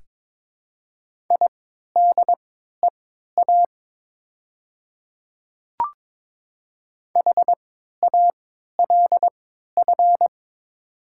Morse code, heard as a keyed tone, about 22 wpm, spaced as for 13 wpm, 700 Hz. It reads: IDEA HALF
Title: 3x Character Spacing - Top 500 Words - 22wpm